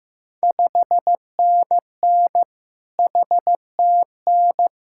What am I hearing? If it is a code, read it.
Morse code, heard as a keyed tone, about 15 wpm, 700 Hz. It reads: 5NN HTN
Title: Ontario QSO Party Exchange 15wpm